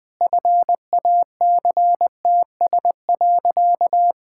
Morse code, heard as a keyed tone, about 20 wpm, 700 Hz. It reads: FACTS.
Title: Sentences with 5-Word Phrases - 20wpm